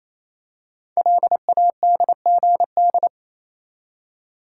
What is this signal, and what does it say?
Morse code, heard as a keyed tone, about 28 wpm, 700 Hz. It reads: LADGD
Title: Code Groups - 5 Letters - 28wpm